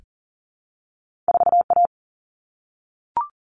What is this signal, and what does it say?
Morse code, heard as a keyed tone, about 38 wpm, 700 Hz. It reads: 4A
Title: Common Cut Numbers - 2-Digit 40wpm